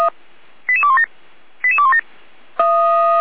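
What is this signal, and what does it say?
Five-tone selective call (ZVEI), heard as quick runs of five tones, tones of about 80 ms. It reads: 88217 88217
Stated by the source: ZVEI five-tone selective call received by SDR.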